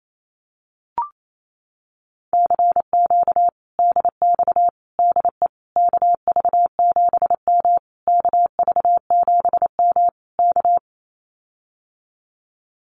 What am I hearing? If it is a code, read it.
Morse code, heard as a keyed tone, about 28 wpm, 700 Hz. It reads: CQ DX DE KV7M KV7M K
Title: Common QSO Phrases Encore 28wpm